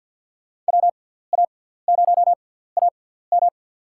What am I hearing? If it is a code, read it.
Morse code, heard as a keyed tone, about 52 wpm, spaced as for 17 wpm, 700 Hz. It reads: WA0AM
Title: USA Call Signs - 5 Characters - 6x Character Spacing - 50wpm